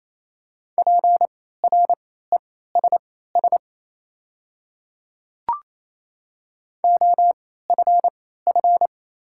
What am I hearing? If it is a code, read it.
Morse code, heard as a keyed tone, about 28 wpm, spaced as for 16 wpm, 700 Hz. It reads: PRESS OFF
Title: 3x Character Spacing - Top 300 Words - 28wpm